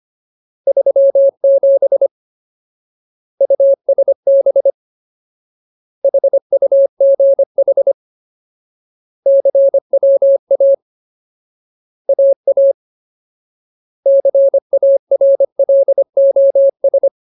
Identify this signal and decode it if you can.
Morse code, heard as a keyed tone, about 25 wpm, 550 Hz. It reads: 37 USB HUGH CWA AA CARLOS
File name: Ultimate Word List / 25WPM Ultimate